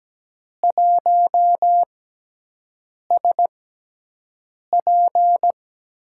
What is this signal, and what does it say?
Morse code, heard as a keyed tone, about 17 wpm, spaced as for 6 wpm, 700 Hz. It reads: 1SP
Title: USA Call Signs - 6x Character Spacing - 17wpm